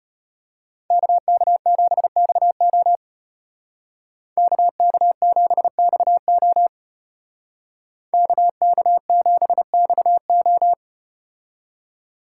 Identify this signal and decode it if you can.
Morse code, heard as a keyed tone, about 34 wpm, 700 Hz. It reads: KK7XO KK7XO KK7XO